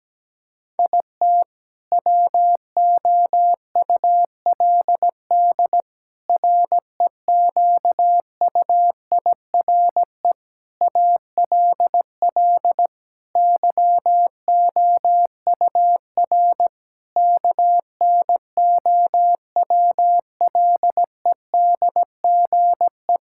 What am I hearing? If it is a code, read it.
Morse code, heard as a keyed tone, about 17 wpm, 700 Hz. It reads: IT WOULD REQUIRE ALL YOUR KNOWLEDGE
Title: Sentences from Top 1000 Words - Review Part 6 - 17wpm